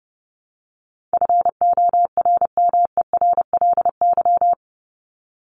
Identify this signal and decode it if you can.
Morse code, heard as a keyed tone, about 30 wpm, 700 Hz. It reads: FORMERLY
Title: LY Suffix - Top 100 Words with Base Words 30wpm